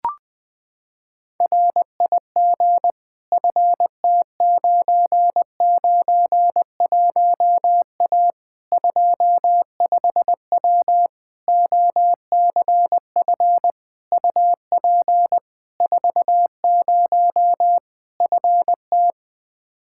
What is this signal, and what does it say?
Morse code, heard as a keyed tone, about 21 wpm, 700 Hz. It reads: RIG FT991A 25W OCF UP 40 FT
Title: Practice QSO - Exchange 2 - Info - 20wpm